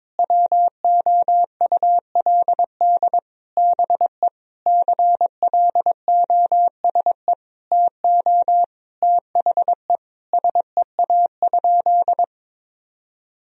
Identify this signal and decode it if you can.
Morse code, heard as a keyed tone, about 22 wpm, 700 Hz. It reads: WOULD BE CLOSE TO THE SEA?
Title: Sentences from Top 1000 Words - Review Part 1 - 22wpm